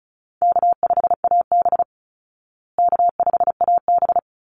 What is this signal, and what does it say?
Morse code, heard as a keyed tone, about 34 wpm, 700 Hz. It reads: K5AB K5AB
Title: Call Signs Repeated 3x - 35wpm